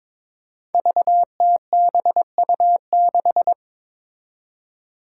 Morse code, heard as a keyed tone, about 22 wpm, 700 Hz. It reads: VTBU6
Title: Five Challenging Letter-Number Combinations 22wpm